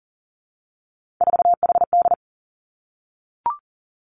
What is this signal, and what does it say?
Morse code, heard as a keyed tone, about 40 wpm, 700 Hz. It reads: 4HD
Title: Three Challenging Letter-Number Combinations 40wpm